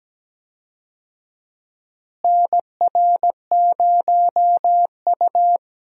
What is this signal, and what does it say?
Morse code, heard as a keyed tone, about 17 wpm, 700 Hz. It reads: NR0U
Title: USA Call Signs - 4 Characters - Repeated 3x - 17wpm